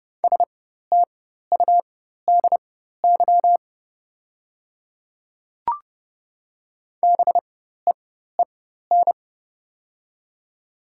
Morse code, heard as a keyed tone, about 31 wpm, spaced as for 14 wpm, 700 Hz. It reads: STUDY BEEN
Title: 4x Character Spacing - Top 300 Words - 30wpm